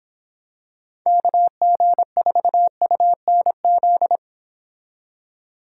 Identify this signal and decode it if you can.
Morse code, heard as a keyed tone, about 26 wpm, 700 Hz. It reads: KG4UNZ